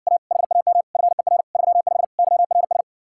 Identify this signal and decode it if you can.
Morse code, heard as a keyed tone, about 70 wpm, 700 Hz. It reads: A HAM FER 35 YRS